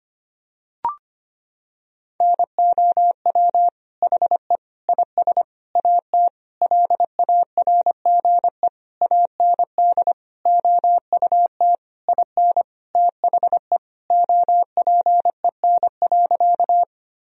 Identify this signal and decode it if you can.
Morse code, heard as a keyed tone, about 25 wpm, 700 Hz. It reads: NOW HE IS AT LARGE AND OUT IN THE OPEN.
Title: Sentences from Top 1000 Words - Review Part 1 - 25wpm